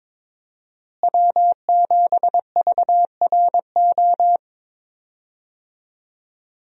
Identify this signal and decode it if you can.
Morse code, heard as a keyed tone, about 22 wpm, 700 Hz. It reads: W7VRO
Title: USA Call Signs - 5 Characters - Repeated 3x - 22wpm